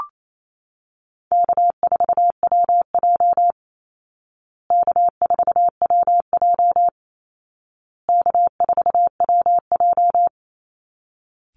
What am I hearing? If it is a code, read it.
Morse code, heard as a keyed tone, about 28 wpm, 700 Hz. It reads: K4WJ K4WJ K4WJ
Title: Easy Call Signs Repeated 3x - 28wpm